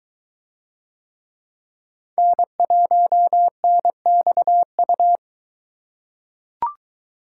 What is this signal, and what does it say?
Morse code, heard as a keyed tone, about 21 wpm, 700 Hz. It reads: N1NXU